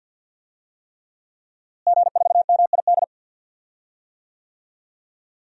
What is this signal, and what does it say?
Morse code, heard as a keyed tone, about 50 wpm, 700 Hz. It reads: G4MID
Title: DXCC-100 Call Signs - 5 Characters - Repeated 3x - 50wpm